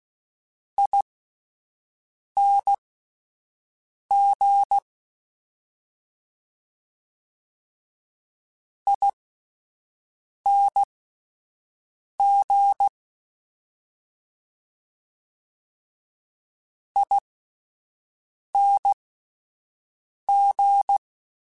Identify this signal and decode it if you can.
Morse code, heard as a keyed tone, about 16 wpm, spaced as for 5 wpm, 800 Hz. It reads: ING ING ING